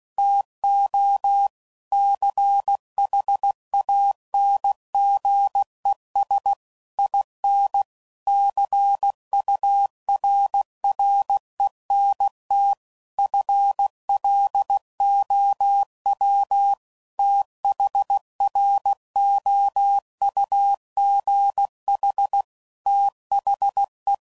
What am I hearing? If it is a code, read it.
Morse code, heard as a keyed tone, about 16 wpm, 800 Hz. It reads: TO CHANGES IN CURRENT FLOW THROUGH THE
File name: CW Code Groups @15WPM / Technical Subjects / Reactance